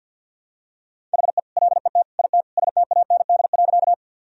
Morse code, heard as a keyed tone, about 52 wpm, 700 Hz. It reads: HE LET IT STAND.
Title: Sentences from Top 1000 Words - Review Part 1 - 50wpm